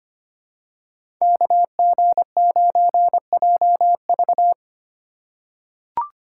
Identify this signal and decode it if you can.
Morse code, heard as a keyed tone, about 25 wpm, 700 Hz. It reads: KG9JV
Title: USA Call Signs - 5 Characters - 2x Character Spacing - 25wpm